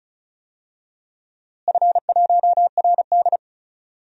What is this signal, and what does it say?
Morse code, heard as a keyed tone, about 36 wpm, 700 Hz. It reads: F1RD
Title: DXCC-100 Call Signs - 4 Characters - Repeated 3x - 35wpm